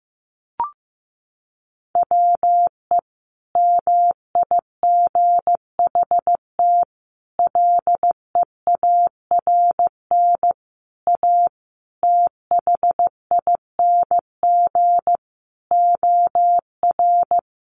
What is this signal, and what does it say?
Morse code, heard as a keyed tone, about 15 wpm, 700 Hz. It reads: WE MIGHT LEARN A THING OR
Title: Sentences from Top 300 Words 15wpm